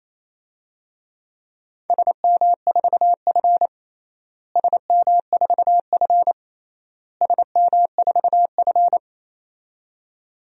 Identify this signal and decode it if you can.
Morse code, heard as a keyed tone, about 27 wpm, 700 Hz. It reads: SM4F SM4F SM4F
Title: DXCC-100 Call Signs - 4 Characters - Repeated 3x - 28wpm